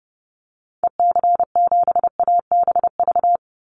Morse code, heard as a keyed tone, about 28 wpm, 700 Hz. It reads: EC7ABV
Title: Call Signs - 30wpm